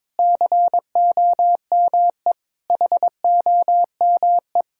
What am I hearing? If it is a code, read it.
Morse code, heard as a keyed tone, about 22 wpm, 700 Hz. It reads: COME HOME
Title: Sentences from Top 1000 Words - Review Part 5 - 22wpm